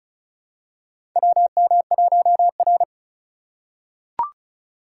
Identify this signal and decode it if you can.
Morse code, heard as a keyed tone, about 36 wpm, 700 Hz. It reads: WM1R
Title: USA Call Signs - 4 Characters - 3x Character Spacing - 35wpm